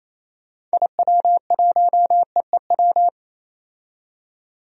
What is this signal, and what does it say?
Morse code, heard as a keyed tone, about 28 wpm, 700 Hz. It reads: IW1EEW
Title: DXCC-100 Call Signs - 6 Characters - 28wpm